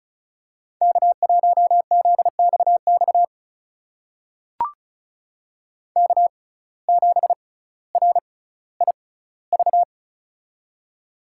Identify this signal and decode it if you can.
Morse code, heard as a keyed tone, about 36 wpm, spaced as for 18 wpm, 700 Hz. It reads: K1ZXX K7RIV
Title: USA Call Signs - 5 Characters - 6x Character Spacing - 35wpm